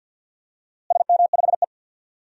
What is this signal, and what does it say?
Morse code, heard as a keyed tone, about 52 wpm, 700 Hz. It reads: SM5E
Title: DXCC-100 Call Signs - 4 Characters - Repeated 2x - 50wpm